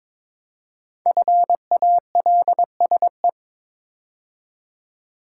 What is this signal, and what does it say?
Morse code, heard as a keyed tone, about 22 wpm, 700 Hz. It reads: FALSE
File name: 5 Letter Words - First 3 Letters Given - 22wpm